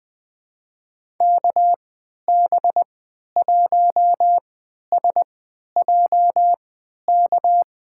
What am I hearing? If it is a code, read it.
Morse code, heard as a keyed tone, about 20 wpm, spaced as for 11 wpm, 700 Hz. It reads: KB1SJK
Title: USA Call Signs - 6 Characters - 3x Character Spacing - 20wpm